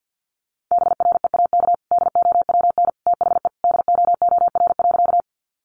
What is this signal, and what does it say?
Morse code, heard as a keyed tone, about 48 wpm, 700 Hz. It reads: BREAK DOWN THE DOOR.
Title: Sentences from Top 900 Words 50wpm